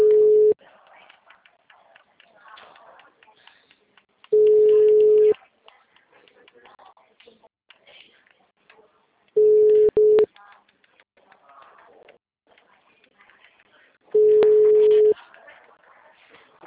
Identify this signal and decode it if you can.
Ringback tone (Europe, 425 Hz), with speech.
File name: out- -2025